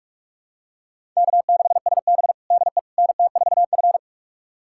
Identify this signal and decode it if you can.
Morse code, heard as a keyed tone, about 46 wpm, 700 Hz. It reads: K6SB DE NT4F
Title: USA Callsign DE Callsign - 4 Characters - 45wpm